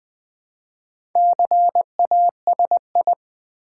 Morse code, heard as a keyed tone, about 20 wpm, 700 Hz. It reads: CASI
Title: Build A Word - 2 Letters At A Time - 6-Letter Words - 20wpm